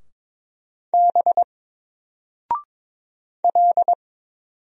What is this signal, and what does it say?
Morse code, heard as a keyed tone, about 22 wpm, 700 Hz. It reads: B L
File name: Single-Letter-Number-22wpm